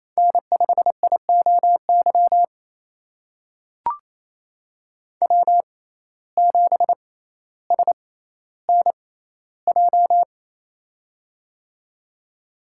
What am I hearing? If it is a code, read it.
Morse code, heard as a keyed tone, about 28 wpm, spaced as for 14 wpm, 700 Hz. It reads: N5IOY W7SNJ
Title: USA Call Signs - 5 Characters - 6x Character Spacing - 28wpm